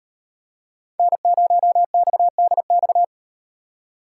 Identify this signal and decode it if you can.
Morse code, heard as a keyed tone, about 38 wpm, 700 Hz. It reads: N0XDX